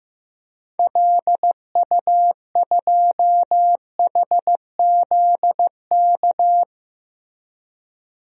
Morse code, heard as a keyed tone, about 15 wpm, 700 Hz. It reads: LU2HZK
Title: DXCC-100 Call Signs - 6 Characters - Repeated 3x - 15wpm